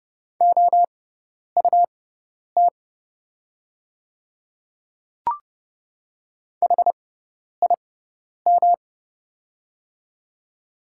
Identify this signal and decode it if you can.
Morse code, heard as a keyed tone, about 30 wpm, spaced as for 10 wpm, 700 Hz. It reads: OUT HIM
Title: 6x Character Spacing - Top 100 Words - 30wpm